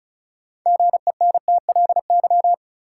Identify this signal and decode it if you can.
Morse code, heard as a keyed tone, about 36 wpm, 700 Hz. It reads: GENTLY
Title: Sentences from Top 1500 Words - New Word Drill Alt - 35wpm